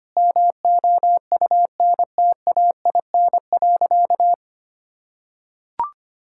Morse code, heard as a keyed tone, about 25 wpm, 700 Hz. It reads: MOUNTAIN.
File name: practice / 25wpm / Sentences with 3-Word Phrases - 25wpm